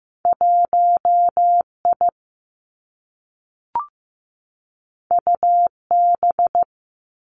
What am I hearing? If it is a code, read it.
Morse code, heard as a keyed tone, about 15 wpm, 700 Hz. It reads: 1I UB
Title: Two Challenging Letter-Number Combinations 15wpm